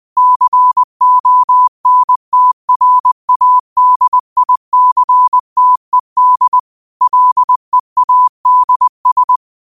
Morse code, heard as a keyed tone, about 20 wpm, 1000 Hz. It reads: CONTRADICTED LEADS
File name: qotdfiles / 20wpm